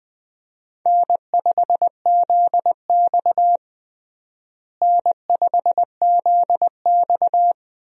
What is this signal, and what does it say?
Morse code, heard as a keyed tone, about 20 wpm, 700 Hz. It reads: N5ZX N5ZX
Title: USA Call Signs - 4 Characters - Repeated 3x - 20wpm